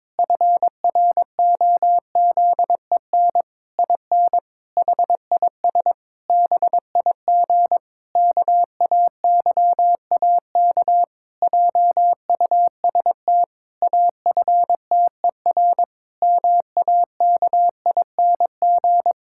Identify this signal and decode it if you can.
Morse code, heard as a keyed tone, about 22 wpm, 700 Hz. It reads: FROZEN IN HIS BIG KAYAK JUST AFTER MAKING
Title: Pangram Challenge v2 - 22wpm